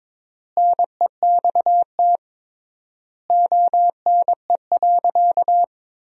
Morse code, heard as a keyed tone, about 22 wpm, 700 Hz. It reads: NEXT ONE.
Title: 3x Word-Spacing - Sentences from Top 400 Words - 22wpm